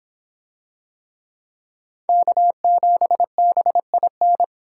Morse code, heard as a keyed tone, about 25 wpm, 700 Hz. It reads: K7BIN